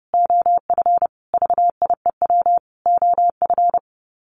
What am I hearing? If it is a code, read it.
Morse code, heard as a keyed tone, about 28 wpm, 700 Hz. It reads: OF VIEW OF
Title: Sets of 5 Words 30wpm